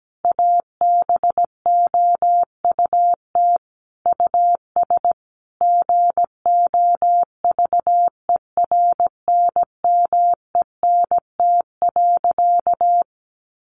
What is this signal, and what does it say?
Morse code, heard as a keyed tone, about 17 wpm, 700 Hz. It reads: ABOUT US GOVERNMENT.
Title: Easy Phrases 17wpm